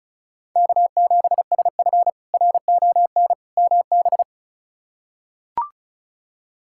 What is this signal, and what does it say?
Morse code, heard as a keyed tone, about 36 wpm, 700 Hz. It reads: K7SF RON MB
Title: K1USN SST QSO Practice - 35wpm